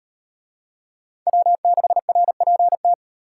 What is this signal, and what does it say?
Morse code, heard as a keyed tone, about 36 wpm, 700 Hz. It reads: W6RPT